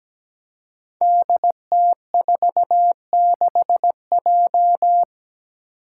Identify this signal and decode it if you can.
Morse code, heard as a keyed tone, about 17 wpm, 700 Hz. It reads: DT46J
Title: Five Challenging Letter-Number Combinations 17wpm